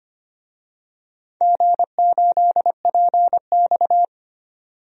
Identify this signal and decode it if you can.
Morse code, heard as a keyed tone, about 25 wpm, 700 Hz. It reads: G8PX
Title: DXCC-100 Call Signs - 4 Characters - 25wpm